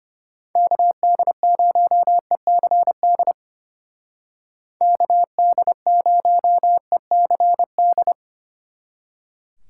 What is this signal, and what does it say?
Morse code, heard as a keyed tone, about 29 wpm, 700 Hz. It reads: KD0ECD KD0ECD
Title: Speed Racing - USA 2x3 Call Signs 25wpm